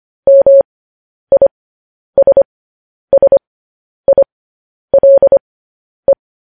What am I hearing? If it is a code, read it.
Morse code, heard as a keyed tone, about 25 wpm, spaced as for 10 wpm, 550 Hz. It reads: MISSILE
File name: podcast / News headlines-1x-25x10